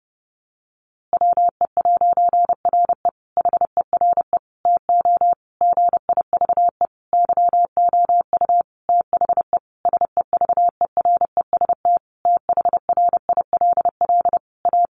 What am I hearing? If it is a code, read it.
Morse code, heard as a keyed tone, about 30 wpm, 700 Hz. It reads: WE'RE HERE TO GIVE YOU THE SEVEREST THRILL A
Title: A Sound Of Thunder 30wpm